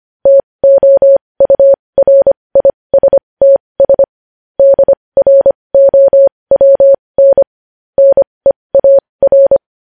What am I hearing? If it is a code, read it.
Morse code, heard as a keyed tone, about 25 wpm, spaced as for 20 wpm, 550 Hz. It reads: TOURISTS DROWN NEAR
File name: podcast / News headlines-1x-25x20